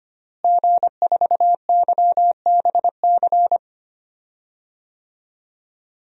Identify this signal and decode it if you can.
Morse code, heard as a keyed tone, about 25 wpm, 700 Hz. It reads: G4YBC
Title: DXCC-100 Call Signs - 5 Characters - Repeated 2x - 25wpm